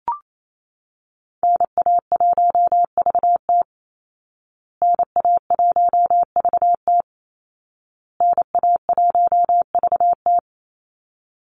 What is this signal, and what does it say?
Morse code, heard as a keyed tone, about 28 wpm, 700 Hz. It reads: NA1VT NA1VT NA1VT
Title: Call Signs Repeated 2x - 28wpm